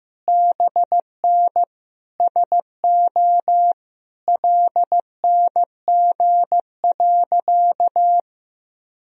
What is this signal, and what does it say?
Morse code, heard as a keyed tone, about 16 wpm, 700 Hz. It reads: BN SO LNG.
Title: Sentences from Top 100 Words and Abbreviations 15wpm